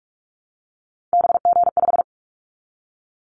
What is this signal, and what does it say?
Morse code, heard as a keyed tone, about 44 wpm, 700 Hz. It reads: BG5
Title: Callsign Prefixes for Top 100 DXCC - 45wpm